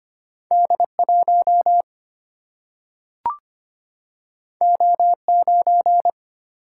Morse code, heard as a keyed tone, about 25 wpm, 700 Hz. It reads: D1 O9
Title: Single-Letter Single-Number 25wpm